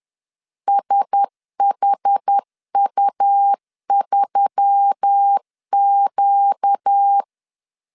Morse code, heard as a keyed tone, about 10 wpm, 800 Hz. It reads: SHU3Q
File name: CW / DET / 1 9 1500C S B 10wpm 4439